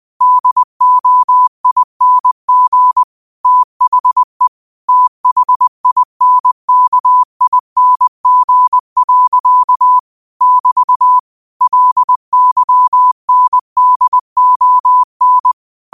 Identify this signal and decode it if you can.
Morse code, heard as a keyed tone, about 20 wpm, 1000 Hz. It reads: DOING THE THINKING. <BT> LYNDON
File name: qotdfiles / 20wpm